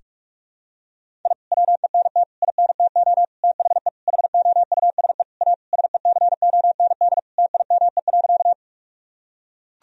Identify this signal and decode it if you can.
Morse code, heard as a keyed tone, about 41 wpm, 700 Hz. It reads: I WENT INTO THE HOUSE A SECOND TIME.
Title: Sentences from Top 1000 Words - Review Part 1 - 45wpm